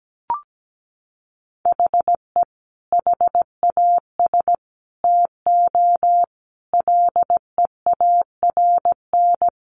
Morse code, heard as a keyed tone, about 17 wpm, 700 Hz. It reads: HE HAS TO LEARN
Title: Sentences from Top 1200 Words 17wpm